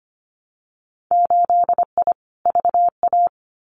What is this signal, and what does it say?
Morse code, heard as a keyed tone, about 25 wpm, 700 Hz. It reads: 8I VA
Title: Winter Field Day Exchanges Encore 25wpm